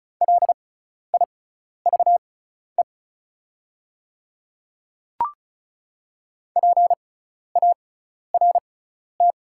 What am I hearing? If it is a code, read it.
Morse code, heard as a keyed tone, about 36 wpm, spaced as for 12 wpm, 690 Hz. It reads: LIVE PART
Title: 6x Character Spacing - Top 300 Words - 35wpm